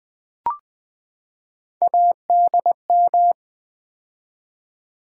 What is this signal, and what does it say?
Morse code, heard as a keyed tone, about 20 wpm, 700 Hz. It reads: ADM
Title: Build A Word - 3 Letters At A Time - 9-Letter Words - 20wpm